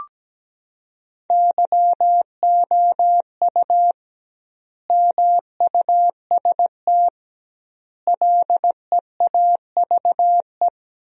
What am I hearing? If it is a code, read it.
Morse code, heard as a keyed tone, about 17 wpm, 700 Hz. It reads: YOU MUST LEAVE
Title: 2x Word-Spacing - Sentences from Top 400 Words - 17wpm